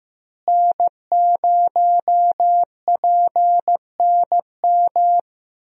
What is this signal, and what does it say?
Morse code, heard as a keyed tone, about 16 wpm, 700 Hz. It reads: N0PNM
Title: USA Call Signs - 5 Characters - Repeated 2x - 15wpm